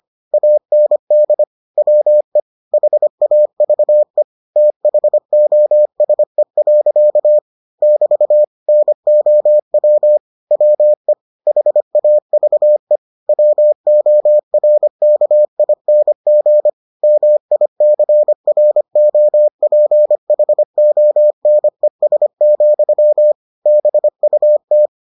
Morse code, heard as a keyed tone, about 25 wpm, 600 Hz. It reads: AND WE HAVE THOSE. <BT> NOW WE HAVE WORKING MICROPHONES, BUT